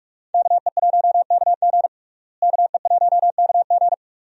Morse code, heard as a keyed tone, about 43 wpm, 700 Hz. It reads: KE1KG KE1KG
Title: USA Call Signs - 5 Characters - Repeated 2x - 45wpm